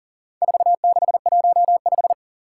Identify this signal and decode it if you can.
Morse code, heard as a keyed tone, about 40 wpm, 690 Hz. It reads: 4615
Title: Four Challenging Letter-Number Combinations 40wpm